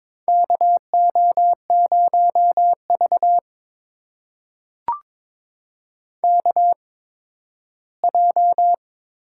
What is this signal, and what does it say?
Morse code, heard as a keyed tone, about 22 wpm, spaced as for 13 wpm, 700 Hz. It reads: KO0V KJ
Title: USA Call Signs - 8x Character Spacing - 22wpm